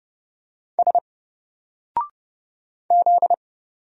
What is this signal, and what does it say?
Morse code, heard as a keyed tone, about 30 wpm, 700 Hz. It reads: S Z
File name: Single-Letter-Number-30wpm